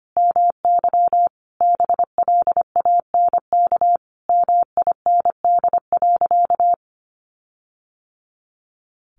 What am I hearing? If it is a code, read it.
Morse code, heard as a keyed tone, about 25 wpm, 700 Hz. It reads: MY BLANK MIND.
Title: Sentences from Top 1500 Words 25wpm